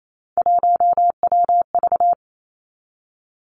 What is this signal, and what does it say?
Morse code, heard as a keyed tone, about 28 wpm, 700 Hz. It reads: 1WV
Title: Three Challenging Letter-Number Combinations 28wpm